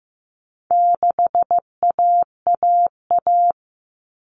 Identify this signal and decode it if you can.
Morse code, heard as a keyed tone, about 15 wpm, 700 Hz. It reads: 6AAA
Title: Common Cut Numbers - 4-Digit 15wpm